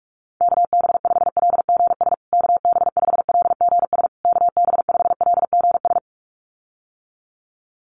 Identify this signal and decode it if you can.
Morse code, heard as a keyed tone, about 44 wpm, 700 Hz. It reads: KB5LGS KB5LGS KB5LGS
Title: USA Call Signs Repeated 3x 45wpm